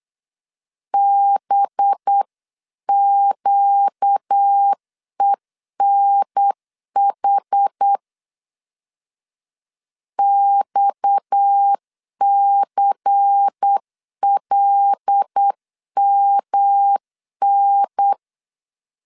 Morse code, heard as a keyed tone, about 8 wpm, 800 Hz. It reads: BQENH XCLMN
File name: CW / DET / 1 6 1500C S C 8wpm 5219